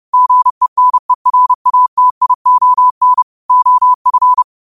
Morse code, heard as a keyed tone, about 30 wpm, 1000 Hz. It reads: GENERATION OF
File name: qotdfiles / 30wpm